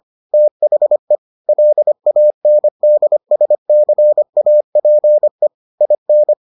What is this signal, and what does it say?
Morse code, heard as a keyed tone, about 25 wpm, 600 Hz. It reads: THE LANDSCAPE IN